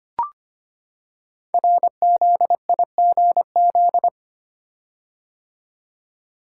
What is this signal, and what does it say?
Morse code, heard as a keyed tone, about 25 wpm, 700 Hz. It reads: RZIGZ
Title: Code Groups - 5 Letters - 25wpm